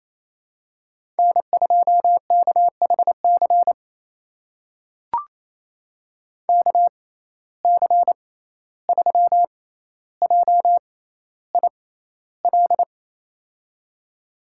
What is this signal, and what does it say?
Morse code, heard as a keyed tone, about 28 wpm, spaced as for 14 wpm, 700 Hz. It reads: N2KHC KC3JIL
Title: USA Call Signs - 6x Character Spacing - 28wpm